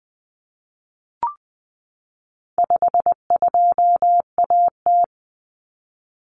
Morse code, heard as a keyed tone, about 20 wpm, 700 Hz. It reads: 52AT
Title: Common Cut Numbers - 4-Digit 20wpm